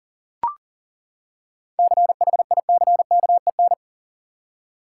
Morse code, heard as a keyed tone, about 40 wpm, 700 Hz. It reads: CHICKEN
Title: Seven Letter Words - 40wpm